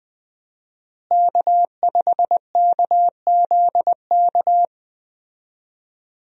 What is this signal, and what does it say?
Morse code, heard as a keyed tone, about 20 wpm, 700 Hz. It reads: K5KZK
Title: USA Call Signs - 5 Characters - Repeated 3x - 20wpm